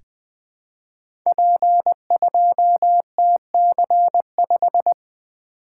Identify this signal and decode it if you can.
Morse code, heard as a keyed tone, about 20 wpm, 700 Hz. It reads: P2TC5
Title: Code Groups - 5 Letter:Numbers - 20wpm